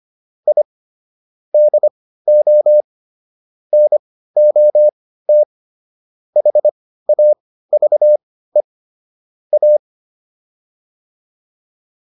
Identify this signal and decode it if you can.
Morse code, heard as a keyed tone, about 25 wpm, spaced as for 15 wpm, 600 Hz. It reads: I DO NOT HAVE A
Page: intermediate-practice-files